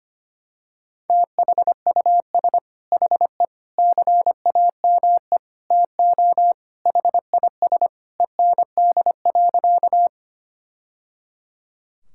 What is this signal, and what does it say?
Morse code, heard as a keyed tone, about 25 wpm, 700 Hz. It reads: THUS HE CAME TO HIS END.
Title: Sentences from Top 1000 Words - Review Part 5 - 25wpm